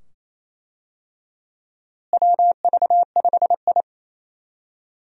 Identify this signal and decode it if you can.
Morse code, heard as a keyed tone, about 28 wpm, 700 Hz. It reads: WV5I
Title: USA Call Signs - 4 Characters - Repeated 2x - 28wpm